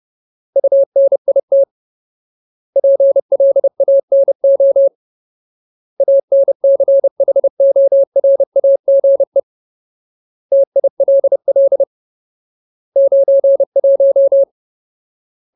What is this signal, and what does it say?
Morse code, heard as a keyed tone, about 30 wpm, 550 Hz. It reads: UNIT PLANO ANCHORAGE TILL 91
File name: Ultimate Word List / 30WPM Ultimate